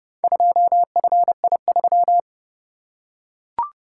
Morse code, heard as a keyed tone, about 28 wpm, 700 Hz. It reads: 2FI3